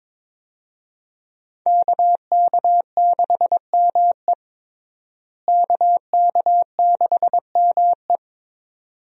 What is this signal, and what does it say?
Morse code, heard as a keyed tone, about 22 wpm, 700 Hz. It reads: KK6ME KK6ME
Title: USA Call Signs - 5 Characters - Repeated 2x - 22wpm